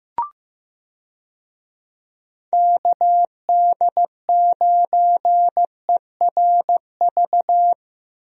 Morse code, heard as a keyed tone, about 15 wpm, 700 Hz. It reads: KD9ERV
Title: USA Call Signs - 6 Characters - Repeated 3x - 15wpm